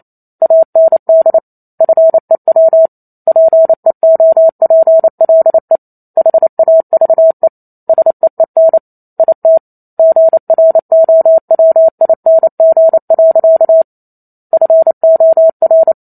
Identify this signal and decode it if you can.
Morse code, heard as a keyed tone, about 28 wpm, 650 Hz. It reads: AND FEW PEOPLE HAVE SEEN IT GROWING. FOR